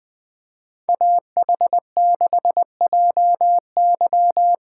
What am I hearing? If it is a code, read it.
Morse code, heard as a keyed tone, about 21 wpm, 700 Hz. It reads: AH6JY